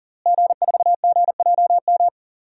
Morse code, heard as a keyed tone, about 40 wpm, 700 Hz. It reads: G4GJM
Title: DXCC-100 Call Signs - 5 Characters - Repeated 2x - 40wpm